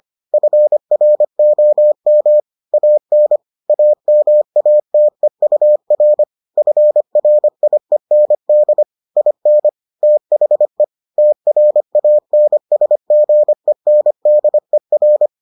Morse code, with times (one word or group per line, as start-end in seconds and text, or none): 0.3-2.4 FROM
2.7-3.4 AN
3.7-6.2 AMATEUR
6.6-8.8 FRIEND
9.2-9.7 IN
10.0-10.8 THE
11.2-15.3 TRANSGENDER